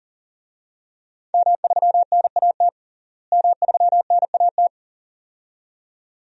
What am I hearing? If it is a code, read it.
Morse code, heard as a keyed tone, about 40 wpm, 700 Hz. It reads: M3NAT M3NAT
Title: DXCC-100 Call Signs - 5 Characters - Repeated 2x - 40wpm